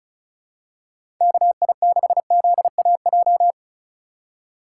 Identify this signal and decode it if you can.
Morse code, heard as a keyed tone, about 34 wpm, 700 Hz. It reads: KI6ZAJ
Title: USA Call Signs - 6 Characters - 2x Character Spacing - 35wpm